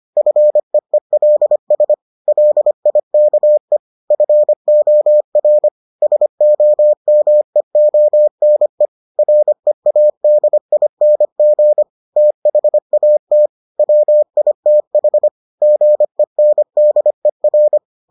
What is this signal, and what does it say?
Morse code, heard as a keyed tone, about 25 wpm, 600 Hz. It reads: FEELS LIKE FOR SOMEONE READING THAT WITH GENDER